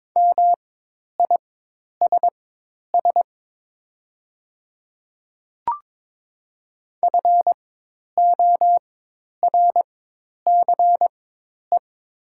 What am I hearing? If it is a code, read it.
Morse code, heard as a keyed tone, about 22 wpm, spaced as for 10 wpm, 700 Hz. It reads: MISS FORCE
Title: 4x Character Spacing - Top 500 Words - 22wpm